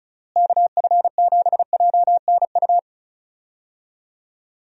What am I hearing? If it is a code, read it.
Morse code, heard as a keyed tone, about 36 wpm, 700 Hz. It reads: KF7JNU
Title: USA Call Signs - 6 Characters - Repeated 2x - 35wpm